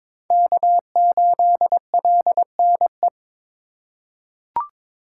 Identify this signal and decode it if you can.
Morse code, heard as a keyed tone, about 22 wpm, 700 Hz. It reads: K8LNE
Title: USA Call Signs - 2x Character Spacing - 22wpm